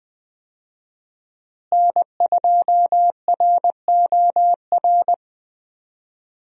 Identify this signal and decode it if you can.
Morse code, heard as a keyed tone, about 20 wpm, 700 Hz. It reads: N2ROR
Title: USA Call Signs - 5 Characters - Repeated 2x - 20wpm